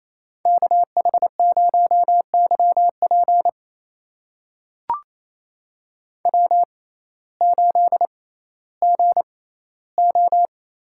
Morse code, heard as a keyed tone, about 28 wpm, spaced as for 15 wpm, 700 Hz. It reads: KH0YP W8GO
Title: USA Call Signs - 5 Characters - 6x Character Spacing - 28wpm